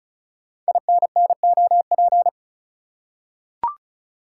Morse code, heard as a keyed tone, about 36 wpm, 690 Hz. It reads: INNOP
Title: Code Groups - 5 Letters - 35wpm